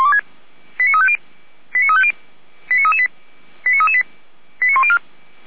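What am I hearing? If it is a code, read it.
Five-tone selective call (ZVEI), heard as quick runs of five tones, tones of about 70 ms. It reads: 89350 89350 89338 89338 89104
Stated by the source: ZVEI five-tone call 89164 received by SDR on a Feuerwehr Orlamünde server.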